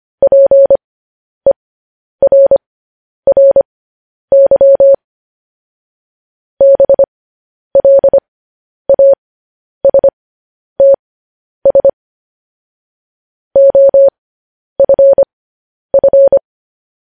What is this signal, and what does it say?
Morse code, heard as a keyed tone, about 25 wpm, spaced as for 10 wpm, 550 Hz. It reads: PERRY BLASTS OFF
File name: podcast / News headlines-1x-25x10